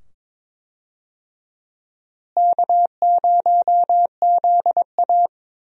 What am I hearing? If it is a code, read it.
Morse code, heard as a keyed tone, about 22 wpm, 700 Hz. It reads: K0ZA
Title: USA Call Signs - 4 Characters - Repeated 2x - 22wpm